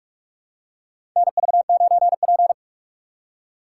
Morse code, heard as a keyed tone, about 47 wpm, 700 Hz. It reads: NV9P
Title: USA Call Signs - 6x Character Spacing - 45wpm